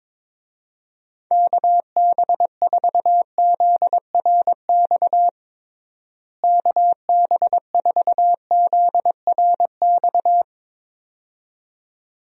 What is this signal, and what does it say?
Morse code, heard as a keyed tone, about 22 wpm, 700 Hz. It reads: KB4ZRX KB4ZRX
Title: USA Call Signs - 6 Characters - Repeated 2x - 22wpm